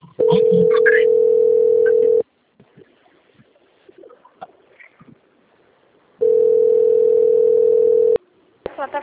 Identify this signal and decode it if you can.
Ringback tone (North America), 440 + 480 Hz, with speech.
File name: out- - - - - .22832